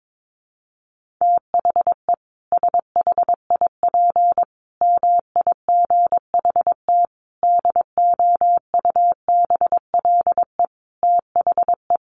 Morse code, heard as a keyed tone, about 22 wpm, 700 Hz. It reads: THE SHIP MIGHT DOUBLE THE
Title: Sentences from Top 1400 Words 22wpm